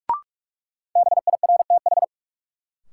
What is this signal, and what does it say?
Morse code, heard as a keyed tone, about 44 wpm, 700 Hz. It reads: BIRTH
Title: Sentences from Top 800 Words - New Word Drill - 45wpm